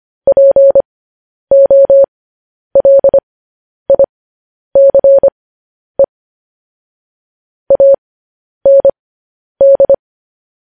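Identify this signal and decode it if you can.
Morse code, heard as a keyed tone, about 25 wpm, spaced as for 10 wpm, 550 Hz. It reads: POLICE AND
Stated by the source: Morse code news headlines at 10 wpm.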